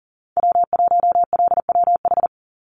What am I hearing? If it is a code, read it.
Morse code, heard as a keyed tone, about 40 wpm, 700 Hz. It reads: W1LWH
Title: Call Signs Repeated 3x - 40wpm